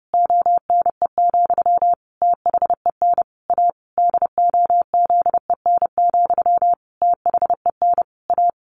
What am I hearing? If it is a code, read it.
Morse code, heard as a keyed tone, about 28 wpm, 700 Hz. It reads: ONE, THEN A DOZEN, THEN A
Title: A Sound Of Thunder 30wpm